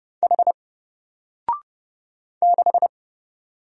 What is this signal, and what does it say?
Morse code, heard as a keyed tone, about 30 wpm, 700 Hz. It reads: H 6
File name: Single-Letter-Number-30wpm